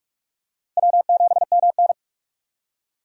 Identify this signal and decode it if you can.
Morse code, heard as a keyed tone, about 47 wpm, 700 Hz. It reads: W7MN
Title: USA Call Signs - 6x Character Spacing - 45wpm